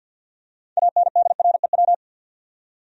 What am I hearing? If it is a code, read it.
Morse code, heard as a keyed tone, about 48 wpm, 700 Hz. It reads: ANDREW
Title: USA Common Names - 50wpm